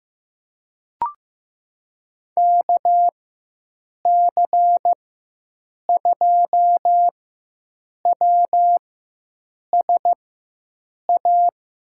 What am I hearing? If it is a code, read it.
Morse code, heard as a keyed tone, about 15 wpm, spaced as for 7 wpm, 700 Hz. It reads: KC2WSA
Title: USA Call Signs - 6 Characters - 4x Character Spacing - 15wpm